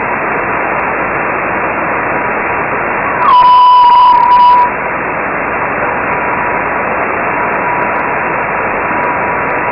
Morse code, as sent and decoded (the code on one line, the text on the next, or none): none